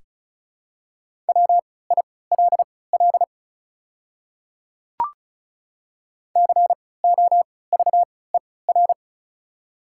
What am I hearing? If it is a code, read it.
Morse code, heard as a keyed tone, about 36 wpm, spaced as for 20 wpm, 700 Hz. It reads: WILL COVER
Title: 3x Character Spacing - Top 500 Words - 35wpm